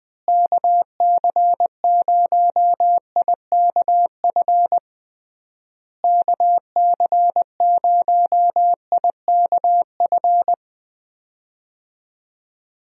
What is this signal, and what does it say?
Morse code, heard as a keyed tone, about 20 wpm, 700 Hz. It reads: KC0IKF KC0IKF
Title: USA Call Signs - 6 Characters - Repeated 3x - 20wpm